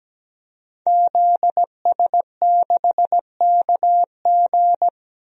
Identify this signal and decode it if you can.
Morse code, heard as a keyed tone, about 17 wpm, 700 Hz. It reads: ZS6KG
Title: DXCC-100 Call Signs - 5 Characters - 17wpm